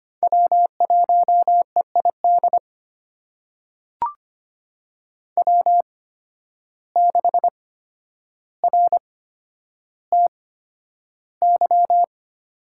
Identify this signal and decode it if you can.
Morse code, heard as a keyed tone, about 25 wpm, spaced as for 11 wpm, 700 Hz. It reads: W1EID W6RTY
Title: USA Call Signs - 5 Characters - 8x Character Spacing - 25wpm